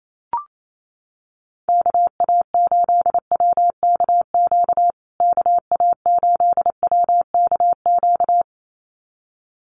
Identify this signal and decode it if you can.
Morse code, heard as a keyed tone, about 28 wpm, 700 Hz. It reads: KA8WKQ KA8WKQ
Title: USA Call Signs Repeated 2x 28wpm